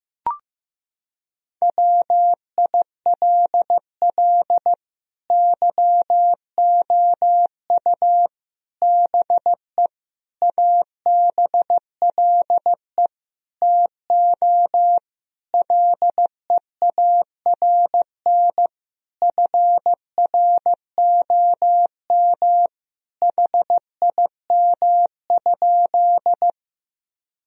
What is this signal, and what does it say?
Morse code, heard as a keyed tone, about 16 wpm, 700 Hz. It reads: WILL YOU BE ABLE TO LEARN FROM HIM?
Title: Sentences from Top 1000 Words - Review Part 2 - 15wpm